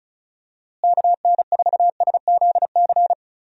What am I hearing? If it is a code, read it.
Morse code, heard as a keyed tone, about 35 wpm, 700 Hz. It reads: KN4SZC
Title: USA Call Signs - 6 Characters - Repeated 2x - 35wpm